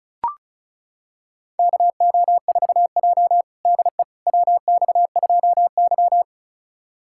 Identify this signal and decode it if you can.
Morse code, heard as a keyed tone, about 34 wpm, 700 Hz. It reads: KO4J DE WX2Y
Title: USA Callsign DE Callsign - 4 Characters - 35wpm